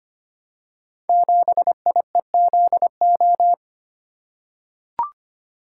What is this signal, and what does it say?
Morse code, heard as a keyed tone, about 25 wpm, 690 Hz. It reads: 7IEZO
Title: Code Groups - 5 Letter:Numbers - 25wpm